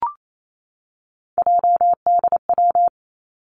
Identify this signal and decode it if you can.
Morse code, heard as a keyed tone, about 28 wpm, 700 Hz. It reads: JDW
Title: Three Challenging Letter-Number Combinations 28wpm